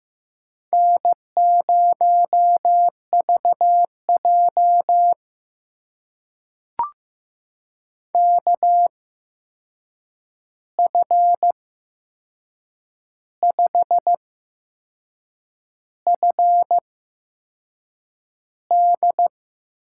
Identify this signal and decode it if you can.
Morse code, heard as a keyed tone, about 15 wpm, spaced as for 6 wpm, 700 Hz. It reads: N0VJ KF5FD
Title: USA Call Signs - 8x Character Spacing - 15wpm